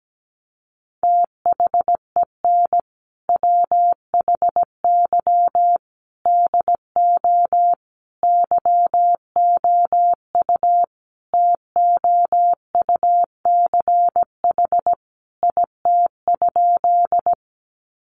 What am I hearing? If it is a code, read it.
Morse code, heard as a keyed tone, about 17 wpm, 700 Hz. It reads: THEN WHY DO YOU TOUCH IT?
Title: Sentences from Top 1200 Words 17wpm